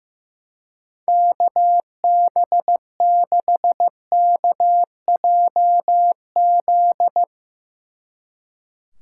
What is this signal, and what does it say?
Morse code, heard as a keyed tone, about 16 wpm, 700 Hz. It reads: KB6KJZ
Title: Speed Racing - USA 2x3 Call Signs 15wpm